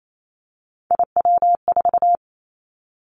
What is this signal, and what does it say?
Morse code, heard as a keyed tone, about 28 wpm, 700 Hz. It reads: IW4
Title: Callsign Prefixes for Top 50 DXCC - 28wpm